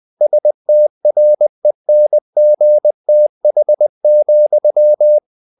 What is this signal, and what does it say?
Morse code, heard as a keyed tone, about 20 wpm, 600 Hz. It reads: STRENGTH,
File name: the old man and the sea part 2 20wpm